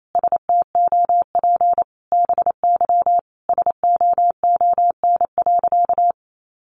Morse code, heard as a keyed tone, about 28 wpm, 700 Hz. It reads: STOP BY SOON.
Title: Sentences from Top 1500 Words 28wpm